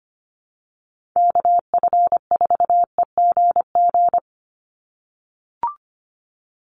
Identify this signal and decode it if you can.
Morse code, heard as a keyed tone, about 25 wpm, 700 Hz. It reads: KF4EGG
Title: USA Call Signs 25wpm